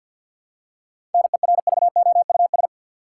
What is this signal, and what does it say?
Morse code, heard as a keyed tone, about 51 wpm, 700 Hz. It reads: NERVOUS
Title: Seven Letter Words - 50wpm